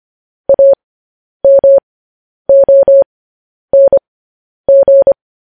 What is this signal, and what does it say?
Morse code, heard as a keyed tone, about 25 wpm, spaced as for 10 wpm, 550 Hz. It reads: AMONG
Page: news headlines in morse code at 10 wpm